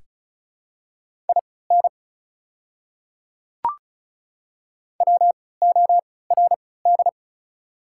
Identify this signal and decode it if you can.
Morse code, heard as a keyed tone, about 36 wpm, spaced as for 20 wpm, 690 Hz. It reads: IN WORD
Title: 3x Character Spacing - Top 100 Words - 35wpm